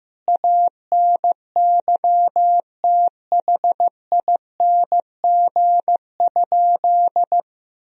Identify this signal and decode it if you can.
Morse code, heard as a keyed tone, about 15 wpm, 700 Hz. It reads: ANYTHING?
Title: Sentences from Top 1000 Words - Review Part 2 - 15wpm